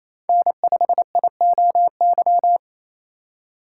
Morse code, heard as a keyed tone, about 28 wpm, 690 Hz. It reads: N5IOY
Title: USA Call Signs - 5 Characters - 4x Character Spacing - 28wpm